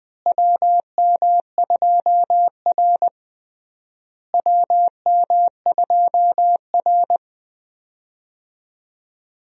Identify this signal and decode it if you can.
Morse code, heard as a keyed tone, about 20 wpm, 700 Hz. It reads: WM2R WM2R
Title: USA Call Signs - 4 Characters - Repeated 3x - 20wpm